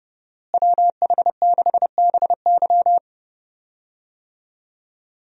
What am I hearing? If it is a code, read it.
Morse code, heard as a keyed tone, about 32 wpm, 700 Hz. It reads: WH6BY
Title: USA Call Signs - 5 Characters - Repeated 2x - 30wpm